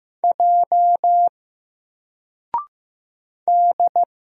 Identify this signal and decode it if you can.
Morse code, heard as a keyed tone, about 15 wpm, 700 Hz. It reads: J D
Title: Single Character (Letter:Number:Punctuation:Prosign) 15wpm